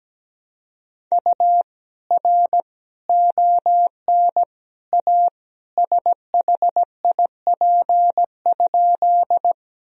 Morse code, heard as a keyed tone, about 17 wpm, 700 Hz. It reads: U R ON A SHIP?
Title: Sentences from Top 400 Words and Abbreviations 17wpm